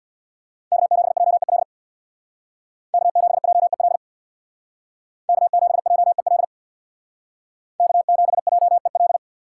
Morse code, heard as a keyed tone, about 58 wpm, 700 Hz. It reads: K7JEL K7JEL K7JEL K7JEL
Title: Speed Racing - USA Call Signs 50wpm